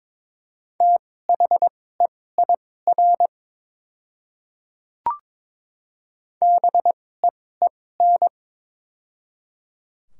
Morse code, heard as a keyed tone, about 22 wpm, spaced as for 16 wpm, 690 Hz. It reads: THEIR BEEN
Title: 2x Character Spacing - Top 100 Words - 22wpm